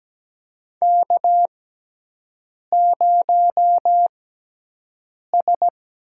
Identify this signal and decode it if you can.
Morse code, heard as a keyed tone, about 17 wpm, spaced as for 6 wpm, 700 Hz. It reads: K0S
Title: USA Call Signs - 4 Characters - 6x Character Spacing - 17wpm